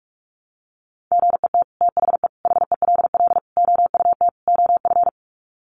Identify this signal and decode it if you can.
Morse code, heard as a keyed tone, about 44 wpm, 700 Hz. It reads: GET THE HELL OUT OF
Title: Sets of 5 Words Encore2 45wpm